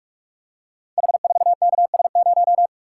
Morse code, heard as a keyed tone, about 44 wpm, 700 Hz. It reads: H4KS0
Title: Code Groups - 5 Letter:Numbers - 45wpm